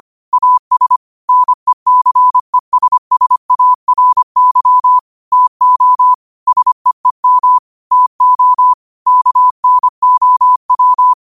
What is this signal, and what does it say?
Morse code, heard as a keyed tone, about 25 wpm, 1000 Hz. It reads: AS NECESSARY TO SEEM TO KNOW
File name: qotdfiles / 25wpm